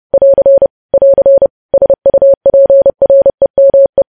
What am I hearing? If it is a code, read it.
Morse code, heard as a keyed tone, about 30 wpm, 550 Hz. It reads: <AR> <AR> SUPREME
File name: podcast / News headlines-1x-30x30